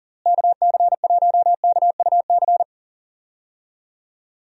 Morse code, heard as a keyed tone, about 39 wpm, 700 Hz. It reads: KC1KUC
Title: USA Call Signs - 6 Characters - Repeated 3x - 40wpm